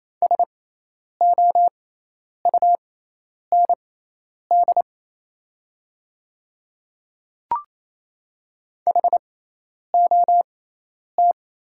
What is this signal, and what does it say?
Morse code, heard as a keyed tone, about 27 wpm, spaced as for 10 wpm, 700 Hz. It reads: SOUND HOT
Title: 6x Character Spacing - Top 200 Words - 28wpm